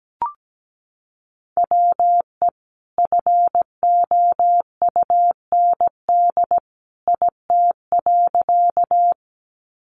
Morse code, heard as a keyed tone, about 17 wpm, 700 Hz. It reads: WE FOUND IT.
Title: Sentences from Top 300 Words 17wpm